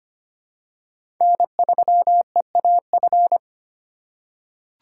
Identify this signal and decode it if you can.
Morse code, heard as a keyed tone, about 25 wpm, 700 Hz. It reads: N3EAF
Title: USA Call Signs - 5 Characters - 3x Character Spacing - 25wpm